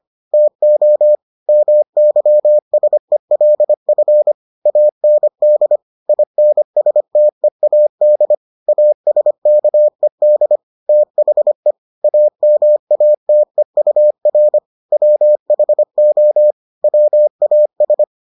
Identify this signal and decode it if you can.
Morse code, heard as a keyed tone, about 25 wpm, 600 Hz. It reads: TO MYSELF AND INSTEAD ASKED THE AMATEUR WHO WAS